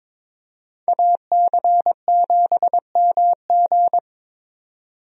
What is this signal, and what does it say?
Morse code, heard as a keyed tone, about 22 wpm, 700 Hz. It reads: AC7MG
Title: USA Call Signs - 5 Characters - 22wpm